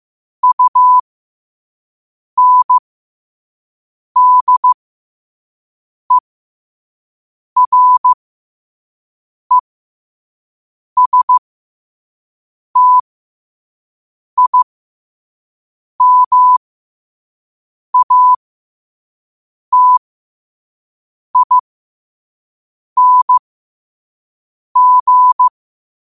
Morse code, heard as a keyed tone, about 15 wpm, spaced as for 5 wpm, 1000 Hz. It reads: UNDERESTIMATING
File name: qotdfiles / 5wpm